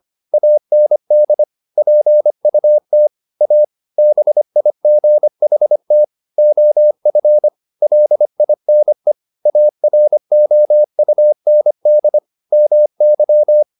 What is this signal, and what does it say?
Morse code, heard as a keyed tone, about 25 wpm, 600 Hz. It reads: AND PUT A BIGHT OF LINE AROUND MY
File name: the old man and the sea part 1 25wpm